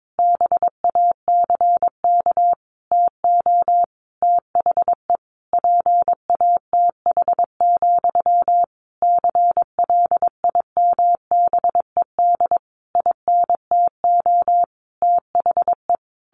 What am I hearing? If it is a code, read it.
Morse code, heard as a keyed tone, about 22 wpm, 700 Hz. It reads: BACK TO THE PATH, CLIMBED INTO THE
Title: A Sound Of Thunder 22wpm